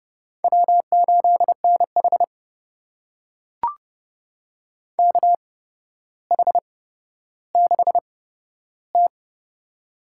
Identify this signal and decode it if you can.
Morse code, heard as a keyed tone, about 31 wpm, spaced as for 13 wpm, 700 Hz. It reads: W8NH KH6T
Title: USA Call Signs - 4 Characters - 8x Character Spacing - 30wpm